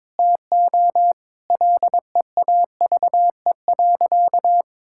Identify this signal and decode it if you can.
Morse code, heard as a keyed tone, about 22 wpm, 700 Hz. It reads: TO LEAVE.
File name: practice / 22wpm / Sentences with 3-Word Phrases - 22wpm